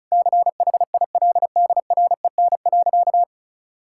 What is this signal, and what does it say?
Morse code, heard as a keyed tone, about 34 wpm, 700 Hz. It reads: CHILDREN.
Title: Sentences from Top 1600 Words 35wpm